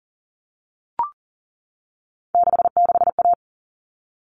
Morse code, heard as a keyed tone, about 40 wpm, 700 Hz. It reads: 66A
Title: All Cut Numbers - 3-Digit 40wpm